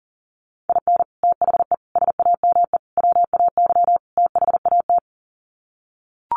Morse code, heard as a keyed tone, about 40 wpm, 700 Hz. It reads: IN THE SAME WAY THAT
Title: Sets of 5 Words 40wpm